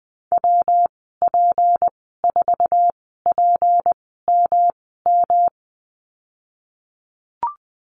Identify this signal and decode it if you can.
Morse code, heard as a keyed tone, about 20 wpm, spaced as for 15 wpm, 700 Hz. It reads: WP4PMM
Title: 167 Lesson 27g - Number 7 - Call Signs 20wpm